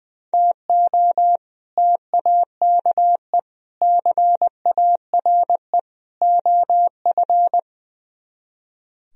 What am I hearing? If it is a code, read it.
Morse code, heard as a keyed tone, about 22 wpm, 700 Hz. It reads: TO TAKE CARE OF
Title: Speed-Racing - Sets of 4 Words 20wpm